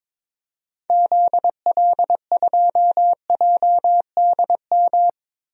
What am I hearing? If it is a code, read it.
Morse code, heard as a keyed tone, about 22 wpm, 700 Hz. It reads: ZL2JDM
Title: DXCC-100 Call Signs - 6 Characters - 22wpm